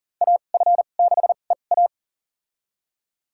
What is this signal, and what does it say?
Morse code, heard as a keyed tone, about 41 wpm, spaced as for 29 wpm, 690 Hz. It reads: AF6EA
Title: USA Call Signs - 2x Character Spacing - 40wpm